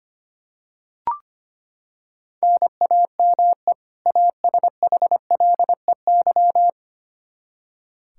Morse code, heard as a keyed tone, about 25 wpm, 700 Hz. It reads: NAME ASHLEY
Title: Practice QSO - Exchange 1 - Beginning - 25wpm